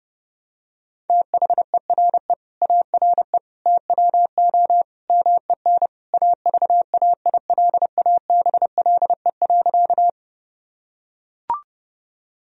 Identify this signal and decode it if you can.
Morse code, heard as a keyed tone, about 32 wpm, 700 Hz. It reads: THERE ARE TWO MEN AVAILABLE.
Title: Sentences from Top 1000 Words - Review Part 3 - 30wpm